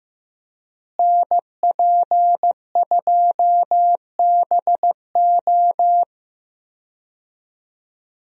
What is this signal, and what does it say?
Morse code, heard as a keyed tone, about 15 wpm, 700 Hz. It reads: NP2BO
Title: USA Call Signs - 5 Characters - Repeated 2x - 15wpm